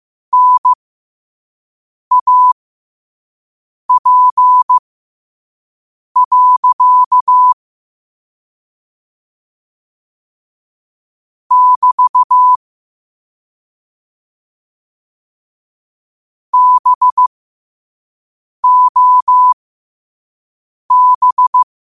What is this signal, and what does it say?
Morse code, heard as a keyed tone, about 15 wpm, spaced as for 5 wpm, 1000 Hz. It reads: NAP. <BT> BOB
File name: qotdfiles / 5wpm